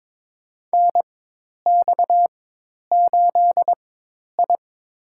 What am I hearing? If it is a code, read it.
Morse code, heard as a keyed tone, about 22 wpm, spaced as for 10 wpm, 700 Hz. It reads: NX8I
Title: USA Call Signs - 4 Characters - 4x Character Spacing - 22wpm